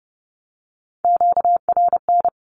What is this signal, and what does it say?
Morse code, heard as a keyed tone, about 30 wpm, 700 Hz. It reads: QRN
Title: Q-Codes 30wpm